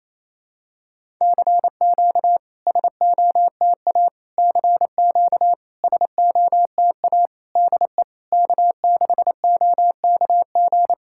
Morse code, heard as a keyed tone, about 27 wpm, 690 Hz. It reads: CQ SOTA CQ SOTA DE K6OKG
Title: SOTA QSO Practice 28wpm